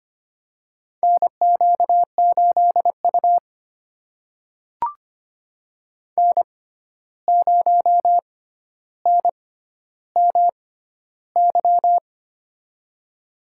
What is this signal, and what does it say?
Morse code, heard as a keyed tone, about 25 wpm, spaced as for 12 wpm, 700 Hz. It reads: NQ8U N0NMY
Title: USA Call Signs - 6x Character Spacing - 25wpm